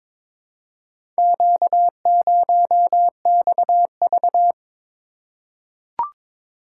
Morse code, heard as a keyed tone, about 22 wpm, 700 Hz. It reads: Q0XV
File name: practice / 22wpm / Four Letter-Number Combinations - 22wpm